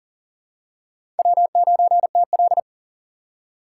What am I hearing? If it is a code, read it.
Morse code, heard as a keyed tone, about 40 wpm, 700 Hz. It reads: W9TL
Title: USA Call Signs - 4 Characters - 3x Character Spacing - 40wpm